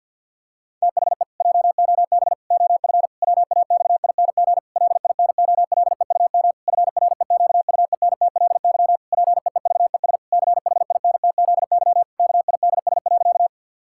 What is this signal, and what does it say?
Morse code, heard as a keyed tone, about 52 wpm, 700 Hz. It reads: THE JOB OF WAXING LINOLEUM FREQUENTLY PEEVES CHINTZY KIDS.
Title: Pangram Challenge v2 - 50wpm